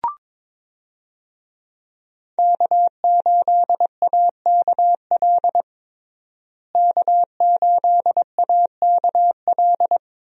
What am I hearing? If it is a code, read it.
Morse code, heard as a keyed tone, about 21 wpm, 700 Hz. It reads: K8AKL K8AKL
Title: USA Call Signs - 5 Characters - Repeated 2x - 22wpm